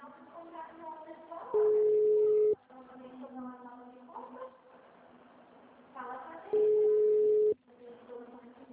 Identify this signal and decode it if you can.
Ringback tone (Europe, 425 Hz), with speech.
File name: out- -2004- - - .24